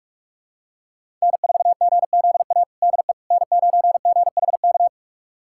Morse code, heard as a keyed tone, about 46 wpm, 700 Hz. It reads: N4GZA DE N9GHK
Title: USA Callsign DE Callsign - 5 Characters - 45wpm